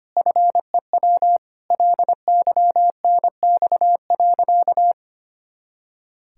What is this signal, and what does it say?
Morse code, heard as a keyed tone, about 25 wpm, 700 Hz. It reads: FEW LYNX.
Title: Pangram Challenge v2 - 25wpm